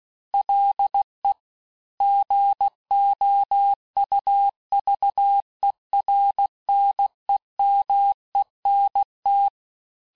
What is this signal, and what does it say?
Morse code, heard as a keyed tone, about 16 wpm, 800 Hz. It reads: LE GOUVERNEMENT
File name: teste-clair-vitesse-15mpm-puis-20mpm